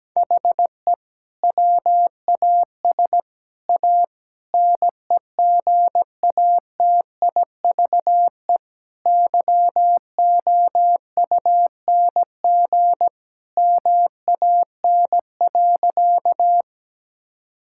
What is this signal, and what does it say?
Morse code, heard as a keyed tone, about 17 wpm, 700 Hz. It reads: HE WAS A NEGATIVE YOUNG MAN.
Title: Sentences from Top 1000 Words - Review Part 6 - 17wpm